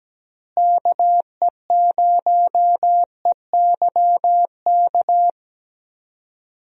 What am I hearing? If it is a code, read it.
Morse code, heard as a keyed tone, about 17 wpm, 700 Hz. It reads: KE0EYK
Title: USA Call Signs - 6 Characters - Repeated 3x - 17wpm